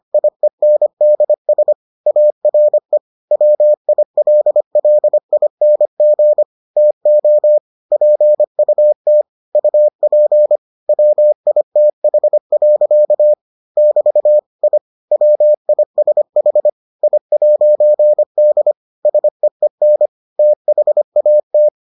Morse code, heard as a keyed tone, about 25 wpm, 600 Hz. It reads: IENDS ARE WILLING TO PUT UP WITH. <BT> I WISH I'D SEEN THAT